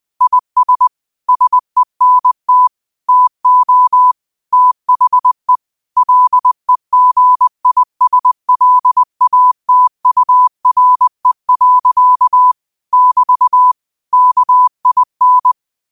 Morse code, heard as a keyed tone, about 20 wpm, 1000 Hz. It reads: IS SENT TO THE LEGISLATURE. <BT> KIN
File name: qotdfiles / 20wpm